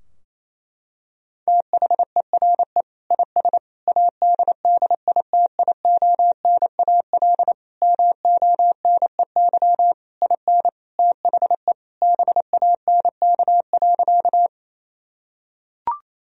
Morse code, heard as a keyed tone, about 28 wpm, 700 Hz. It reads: THERE IS ADDITIONAL MONEY IN THE BANK.
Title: Sentences from Top 1000 Words - Review Part 4 - 28wpm